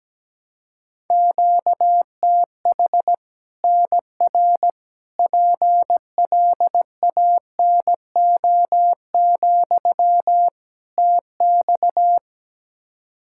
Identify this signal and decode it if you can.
Morse code, heard as a keyed tone, about 17 wpm, 700 Hz. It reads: QTH NR PLANO, TX
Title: Practice QSO - Exchange 1 - Beginning - 17wpm